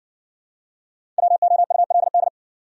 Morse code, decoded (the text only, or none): WOULD